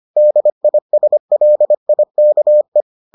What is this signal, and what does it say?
Morse code, heard as a keyed tone, about 25 wpm, 600 Hz. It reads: DISLIKE